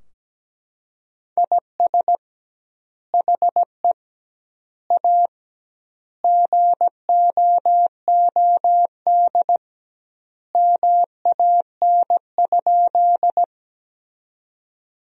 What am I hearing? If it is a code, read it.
Morse code, heard as a keyed tone, about 17 wpm, 700 Hz. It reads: IS HE A GOOD MAN?
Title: 2x Word-Spacing - Sentences from Top 200 Words - 17wpm